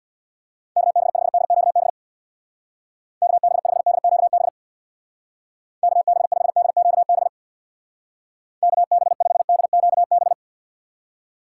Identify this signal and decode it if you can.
Morse code, heard as a keyed tone, about 64 wpm, 700 Hz. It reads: KB5DQB KB5DQB KB5DQB KB5DQB